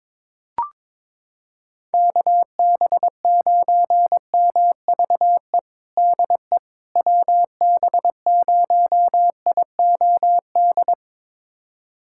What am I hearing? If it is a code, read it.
Morse code, heard as a keyed tone, about 22 wpm, 700 Hz. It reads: KB9MVE DE WB0IOD
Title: USA Callsign DE Callsign 22wpm